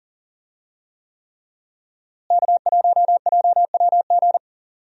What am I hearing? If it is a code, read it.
Morse code, heard as a keyed tone, about 37 wpm, 700 Hz. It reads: K1JWG
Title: USA Call Signs - 5 Characters - Repeated 3x - 40wpm